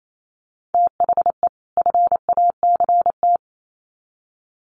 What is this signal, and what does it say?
Morse code, heard as a keyed tone, about 28 wpm, 700 Hz. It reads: THE FACT
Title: Sets of 2 Words 28wpm